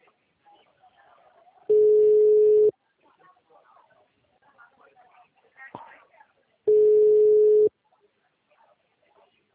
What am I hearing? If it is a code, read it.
Ringback tone (Europe, 425 Hz), with speech.
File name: out- -2015